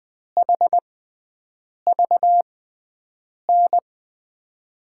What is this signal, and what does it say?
Morse code, heard as a keyed tone, about 20 wpm, spaced as for 7 wpm, 700 Hz. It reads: HVN
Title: USA Call Signs - 5 Characters - 6x Character Spacing - 20wpm